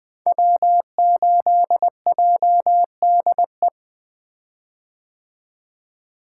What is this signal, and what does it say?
Morse code, heard as a keyed tone, about 20 wpm, 700 Hz. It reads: W8JDE
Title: 2023 POTA Activator Call Signs - 20wpm